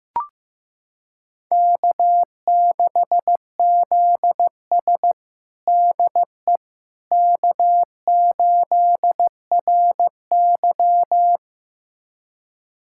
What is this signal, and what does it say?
Morse code, heard as a keyed tone, about 16 wpm, 700 Hz. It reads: K6ZS DE K8RY
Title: USA Callsign DE Callsign - 4 Characters - 15wpm